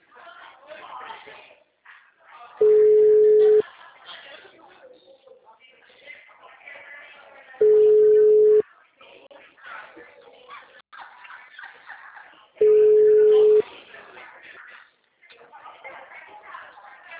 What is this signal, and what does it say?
Ringback tone (Europe, 425 Hz), with speech.